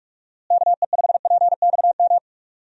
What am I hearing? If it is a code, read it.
Morse code, heard as a keyed tone, about 41 wpm, 700 Hz. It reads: KE5PXM
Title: USA Call Signs - 6 Characters - Repeated 3x - 45wpm